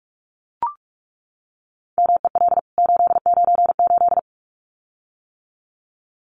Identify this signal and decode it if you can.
Morse code, heard as a keyed tone, about 44 wpm, 700 Hz. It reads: MEL 898
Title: CWT Exchange 45wpm